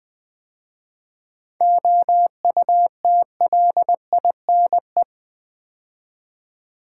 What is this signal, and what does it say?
Morse code, heard as a keyed tone, about 20 wpm, 700 Hz. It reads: OUTLINE
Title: 7 Letter Words - First 3 Letters Given - 20wpm